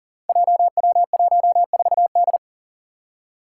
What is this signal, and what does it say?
Morse code, heard as a keyed tone, about 40 wpm, 700 Hz. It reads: JW14D
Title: Five Challenging Letter-Number Combinations 40wpm